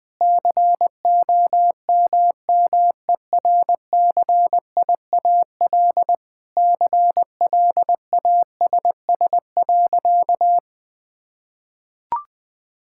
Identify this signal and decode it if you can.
Morse code, heard as a keyed tone, about 20 wpm, 700 Hz. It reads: COMMERCIAL CLASS.
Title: Sentences from Top 1000 Words - Review Part 5 - 20wpm